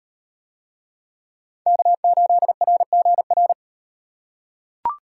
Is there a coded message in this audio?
Yes: Morse code from 1.7 to 3.5 s, heard as a keyed tone, about 38 wpm, 700 Hz. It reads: K8RGR